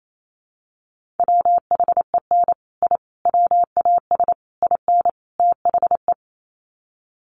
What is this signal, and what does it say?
Morse code, heard as a keyed tone, about 28 wpm, 700 Hz. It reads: WHEN I WAS IN THE
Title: Sets of 5 Words Encore2 28wpm